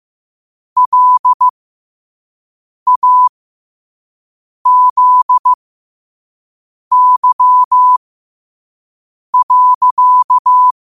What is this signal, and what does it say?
Morse code, heard as a keyed tone, about 15 wpm, spaced as for 5 wpm, 1000 Hz. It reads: LAZY.
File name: qotdfiles / 5wpm